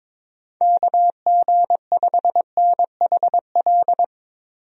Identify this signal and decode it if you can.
Morse code, heard as a keyed tone, about 22 wpm, 700 Hz. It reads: KG5NHL
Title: USA Call Signs - 6 Characters - Repeated 2x - 22wpm